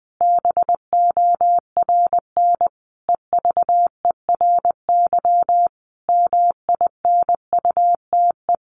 Morse code, heard as a keyed tone, about 20 wpm, 700 Hz. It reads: BORN EVERY MINUTE
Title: Sentences from Top 1400 Words 20wpm